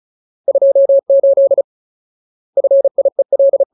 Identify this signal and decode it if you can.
Morse code, heard as a keyed tone, about 36 wpm, 550 Hz. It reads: 28 FIEL
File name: Ultimate Word List / 35WPM Ultimate